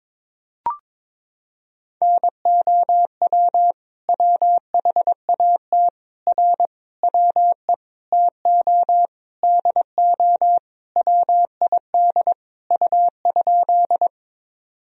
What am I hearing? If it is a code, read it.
Morse code, heard as a keyed tone, about 22 wpm, 700 Hz. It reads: NOW WHAT R WE TO DO WID U?
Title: Sentences from Top 100 Words and Abbreviations 22wpm